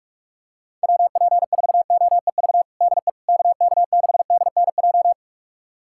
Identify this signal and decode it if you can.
Morse code, heard as a keyed tone, about 42 wpm, 700 Hz. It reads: WP4OEV DE KK6DNJ
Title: USA Callsign DE Callsign - 6 Characters - 45wpm